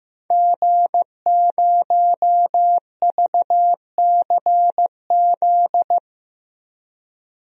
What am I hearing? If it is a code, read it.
Morse code, heard as a keyed tone, about 15 wpm, 700 Hz. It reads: G0VCZ
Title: DXCC-100 Call Signs - 5 Characters - 15wpm